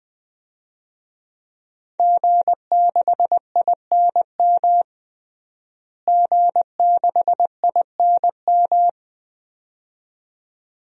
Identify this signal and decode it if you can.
Morse code, heard as a keyed tone, about 20 wpm, 700 Hz. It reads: G6INM G6INM
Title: DXCC-100 Call Signs - 5 Characters - Repeated 2x - 20wpm